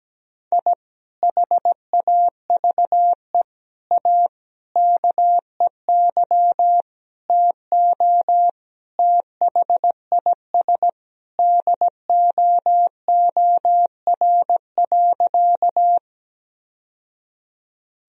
Morse code, heard as a keyed tone, about 17 wpm, 700 Hz. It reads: I HAVE A KEY TO THIS DOOR.
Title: Sentences from Top 1000 Words - Review Part 4 - 17wpm